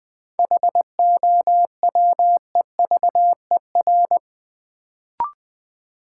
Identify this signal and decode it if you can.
Morse code, heard as a keyed tone, about 20 wpm, 700 Hz. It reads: HOWEVER
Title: Sentences from Top 600 Words - New Word Drill - 20wpm